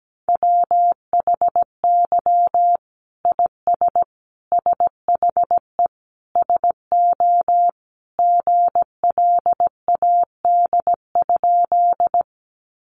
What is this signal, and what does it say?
Morse code, heard as a keyed tone, about 17 wpm, 700 Hz. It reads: WHY IS SHE SO GLAD?
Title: Sentences from Top 800 Words 17wpm